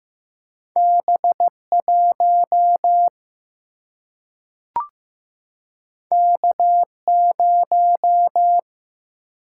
Morse code, heard as a keyed tone, about 15 wpm, 700 Hz. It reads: B 1 K 0
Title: Single-Letter Single-Number 15wpm